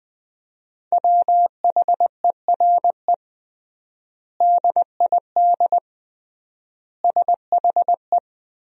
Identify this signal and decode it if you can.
Morse code, heard as a keyed tone, about 20 wpm, 700 Hz. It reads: WHERE DID SHE
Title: 3x Word-Spacing - Sentences from Top 200 Words - 20wpm